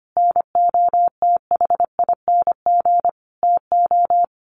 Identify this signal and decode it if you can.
Morse code, heard as a keyed tone, about 25 wpm, 700 Hz. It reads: NOTHING TO
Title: Sentences from Top 1100 Words 25wpm